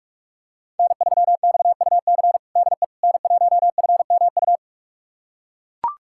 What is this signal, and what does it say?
Morse code, heard as a keyed tone, about 42 wpm, 700 Hz. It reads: N3XUC DE N1FMU
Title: USA Callsign DE Callsign - 5 Characters - 45wpm